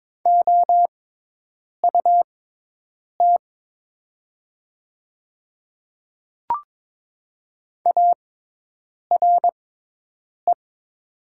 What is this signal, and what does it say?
Morse code, heard as a keyed tone, about 22 wpm, spaced as for 8 wpm, 700 Hz. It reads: OUT ARE
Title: 6x Character Spacing - Top 100 Words - 22wpm